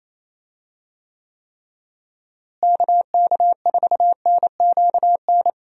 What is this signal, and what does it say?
Morse code, heard as a keyed tone, about 28 wpm, 700 Hz. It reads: KK4NQN
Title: USA Call Signs - 6 Characters - Repeated 3x - 28wpm